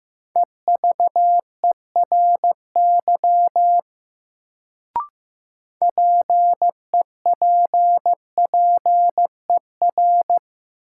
Morse code, heard as a keyed tone, about 15 wpm, 700 Hz. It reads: EVERY PEPPER
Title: Sets of 5 Words - New Word Drill - 15wpm